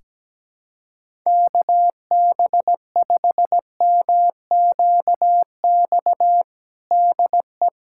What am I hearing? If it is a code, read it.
Morse code, heard as a keyed tone, about 17 wpm, 700 Hz. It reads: KB5MQX DE
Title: USA Callsign DE Callsign 17wpm